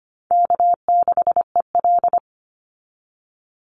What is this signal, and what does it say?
Morse code, heard as a keyed tone, about 25 wpm, 700 Hz. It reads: K6EL
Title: NA SOTA Chaser Callsigns 25wpm